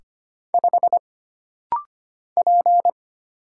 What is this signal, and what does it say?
Morse code, heard as a keyed tone, about 25 wpm, 700 Hz. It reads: 5 P
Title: Single Letter-Number - Rapid-Fire 25wpm